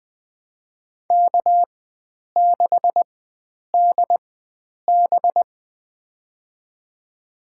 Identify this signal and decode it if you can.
Morse code, heard as a keyed tone, about 20 wpm, spaced as for 9 wpm, 700 Hz. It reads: K6DB
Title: USA Call Signs - 4 Characters - 4x Character Spacing - 20wpm